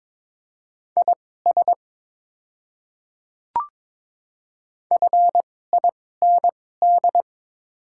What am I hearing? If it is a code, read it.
Morse code, heard as a keyed tone, about 22 wpm, spaced as for 16 wpm, 700 Hz. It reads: IS FIND
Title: 2x Character Spacing - Top 100 Words - 22wpm